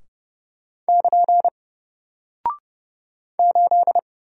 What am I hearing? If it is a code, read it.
Morse code, heard as a keyed tone, about 30 wpm, 700 Hz. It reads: <KN> 8
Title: Single Character (Letter:Number:Punctuation:Prosign) Alt 30wpm